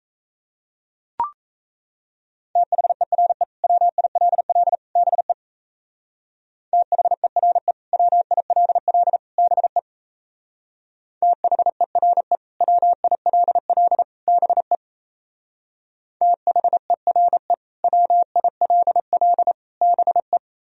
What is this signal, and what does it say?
Morse code, heard as a keyed tone, about 35 wpm, 700 Hz. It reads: THERE WILL BE THERE WILL BE THERE WILL BE THERE WILL BE
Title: Speed-Racing - Sets of 3 Words 28wpm